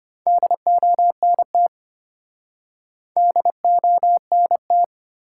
Morse code, heard as a keyed tone, about 33 wpm, 700 Hz. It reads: DONT DONT